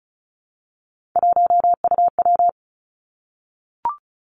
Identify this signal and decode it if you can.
Morse code, heard as a keyed tone, about 36 wpm, 700 Hz. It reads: 1UW
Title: Three Challenging Letter-Number Combinations 35wpm